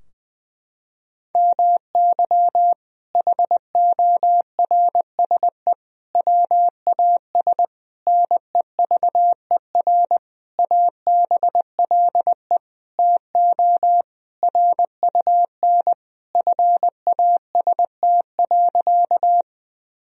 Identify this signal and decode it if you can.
Morse code, heard as a keyed tone, about 20 wpm, 700 Hz. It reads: MY HORSE WAS NEVER ABLE TO RUN FAST.
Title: Sentences from Top 1000 Words - Review Part 2 - 20wpm